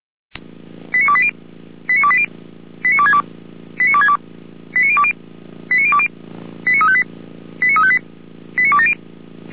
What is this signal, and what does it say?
Five-tone selective call (ZVEI), heard as quick runs of five tones, tones of about 70 ms. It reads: 89280 89280 89262 89262 89020 89020 89358 89358 89270